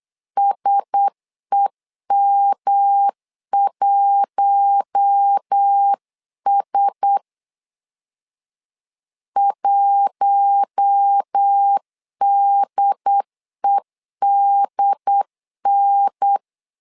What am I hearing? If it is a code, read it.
Morse code, heard as a keyed tone, about 8 wpm, 800 Hz. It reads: SEM1S 1DEDN
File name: LEZIONI / 4 Q1S 8wpm 2710